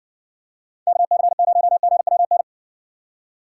Morse code, heard as a keyed tone, about 56 wpm, 700 Hz. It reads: KC0GWN